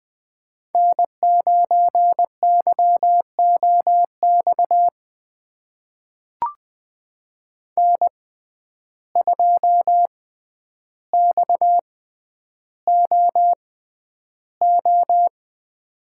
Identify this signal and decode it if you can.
Morse code, heard as a keyed tone, about 20 wpm, spaced as for 10 wpm, 700 Hz. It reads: N9YOX N2XOO
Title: USA Call Signs - 5 Characters - 6x Character Spacing - 20wpm